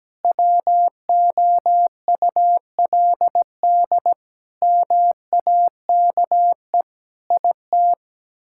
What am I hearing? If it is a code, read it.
Morse code, heard as a keyed tone, about 17 wpm, 700 Hz. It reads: WOULD MAKE IT
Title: Sentences from Top 1000 Words - Review Part 6 - 17wpm